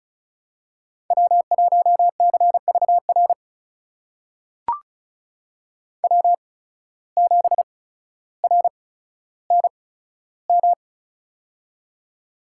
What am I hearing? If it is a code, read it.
Morse code, heard as a keyed tone, about 36 wpm, spaced as for 15 wpm, 700 Hz. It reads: W1CVR W7RNM
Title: USA Call Signs - 5 Characters - 8x Character Spacing - 35wpm